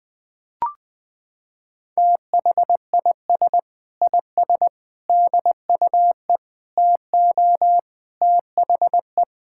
Morse code, heard as a keyed tone, about 20 wpm, 700 Hz. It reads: THIS IS DUE TO THE
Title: Sets of 8 Words - 20wpm